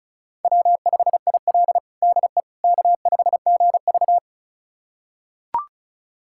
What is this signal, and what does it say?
Morse code, heard as a keyed tone, about 36 wpm, 700 Hz. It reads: W5IL DE K5GV
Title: USA Callsign DE Callsign - 4 Characters - 35wpm